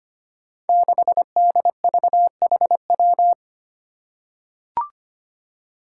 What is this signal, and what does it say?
Morse code, heard as a keyed tone, about 25 wpm, 700 Hz. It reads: 6DVHW
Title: Five Challenging Letter-Number Combinations 25wpm